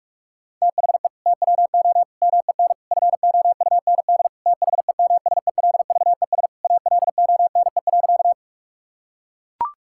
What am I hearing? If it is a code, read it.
Morse code, heard as a keyed tone, about 41 wpm, 700 Hz. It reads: THE TWO MEN FOUND THEMSELVES ALONE.
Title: Sentences from Top 1000 Words - Review Part 5 - 45wpm